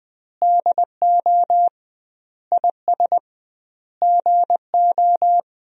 Morse code, heard as a keyed tone, about 20 wpm, 700 Hz. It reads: DO IS GO
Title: 2x Word-Spacing - Sentences from Top 200 Words - 20wpm